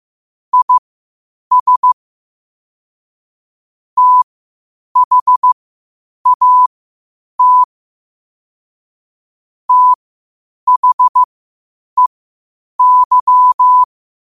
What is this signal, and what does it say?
Morse code, heard as a keyed tone, about 15 wpm, spaced as for 8 wpm, 1000 Hz. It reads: IS THAT THEY
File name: qotdfiles / 8wpm